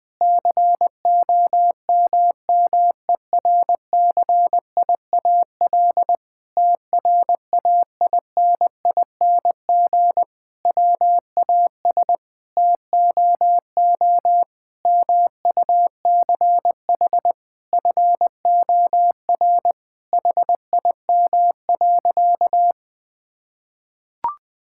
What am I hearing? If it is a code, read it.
Morse code, heard as a keyed tone, about 20 wpm, 700 Hz. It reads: COMMERCIAL TRAINING WAS TOO MUCH FOR HIM.
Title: Sentences from Top 1000 Words - Review Part 5 - 20wpm